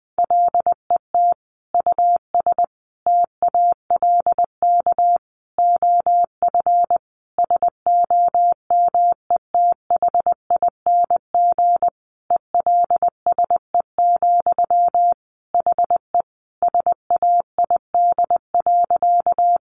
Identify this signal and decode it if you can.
Morse code, heard as a keyed tone, about 19 wpm, 700 Hz. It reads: LET US TALK OF SOMETHING ELSE, HE SAID.
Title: Sentences from Top 800 Words 20wpm